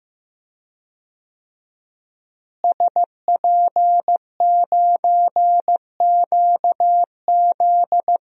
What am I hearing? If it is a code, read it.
Morse code, heard as a keyed tone, about 15 wpm, 700 Hz. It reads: SP9QZ
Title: DXCC-100 Call Signs - 5 Characters - Repeated 3x - 15wpm